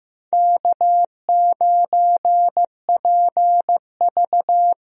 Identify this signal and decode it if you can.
Morse code, heard as a keyed tone, about 16 wpm, 700 Hz. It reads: K9PV
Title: USA Call Signs - 4 Characters - Repeated 2x - 15wpm